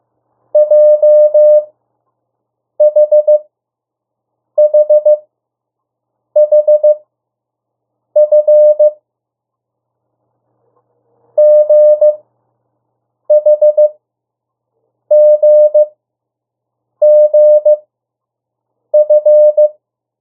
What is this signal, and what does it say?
Morse code, heard as a keyed tone, about 15 wpm, spaced as for 6 wpm, 600 Hz. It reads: JHHHF GHGGF
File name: morse mp3 / 02 urok 1